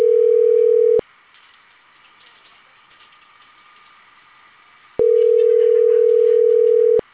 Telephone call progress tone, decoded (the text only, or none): ringback tone